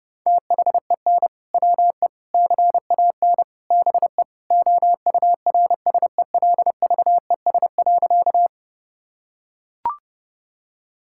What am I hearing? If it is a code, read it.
Morse code, heard as a keyed tone, about 30 wpm, 700 Hz. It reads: THEN WE CAN BE OURSELVES.
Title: Sentences from Top 1600 Words 30wpm